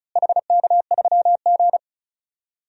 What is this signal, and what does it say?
Morse code, heard as a keyed tone, about 36 wpm, 700 Hz. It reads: HK3G
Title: DXCC-100 Call Signs - 4 Characters - Repeated 3x - 35wpm